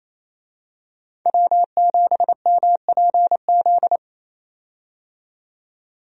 Morse code, heard as a keyed tone, about 27 wpm, 700 Hz. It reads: W7MPZ
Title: USA Call Signs - 5 Characters - 28wpm